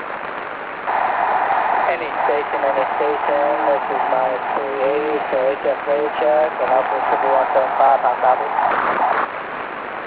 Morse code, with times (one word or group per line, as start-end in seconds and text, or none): none